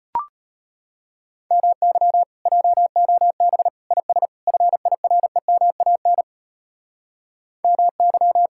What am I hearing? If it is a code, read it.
Morse code, heard as a keyed tone, about 33 wpm, 700 Hz. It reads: MY JOB IS FIREMAN MY